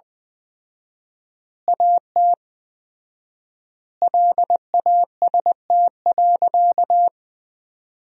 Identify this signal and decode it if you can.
Morse code, heard as a keyed tone, about 20 wpm, 700 Hz. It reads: AT LAST.
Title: 4x Word-Spacing - Sentences from Top 300 Words - 20wpm